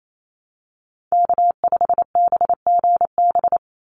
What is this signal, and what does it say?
Morse code, heard as a keyed tone, about 28 wpm, 690 Hz. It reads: K5BGB
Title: Call Signs Repeated 3x - 28wpm